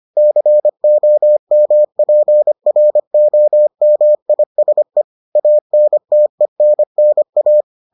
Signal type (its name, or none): Morse code